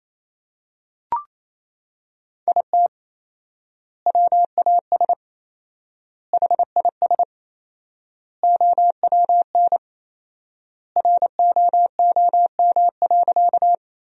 Morse code, heard as a keyed tone, about 27 wpm, 700 Hz. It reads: IT WAS HIS OWN ROOM.
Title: 4x Word-Spacing - Sentences from Top 300 Words - 28wpm